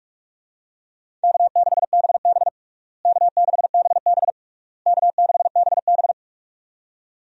Morse code, heard as a keyed tone, about 43 wpm, 700 Hz. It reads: K6BB K6BB K6BB
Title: USA Call Signs - 4 Characters - Repeated 3x - 45wpm